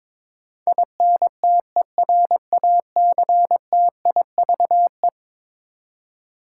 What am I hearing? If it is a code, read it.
Morse code, heard as a keyed tone, about 22 wpm, 700 Hz. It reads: INTERACTIVE
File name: practice / 22wpm / Eleven Letter Words - 22wpm